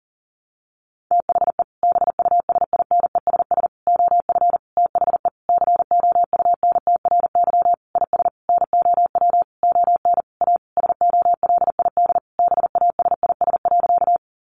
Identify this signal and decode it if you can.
Morse code, heard as a keyed tone, about 40 wpm, 700 Hz. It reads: THE BUSINESS OF THE COUNTRY IS NOW ON A SOLID BASIS.
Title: Sentences from Top 1500 Words 40wpm